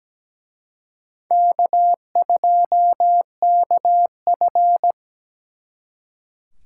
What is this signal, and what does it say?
Morse code, heard as a keyed tone, about 20 wpm, 700 Hz. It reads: K2KF
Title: Speed Racing - USA 1x2 Call Signs 17wpm